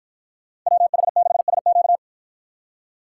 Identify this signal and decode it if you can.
Morse code, heard as a keyed tone, about 50 wpm, 700 Hz. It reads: WH6SQ